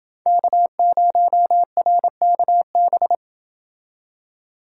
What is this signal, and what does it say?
Morse code, heard as a keyed tone, about 27 wpm, 700 Hz. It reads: K0RKB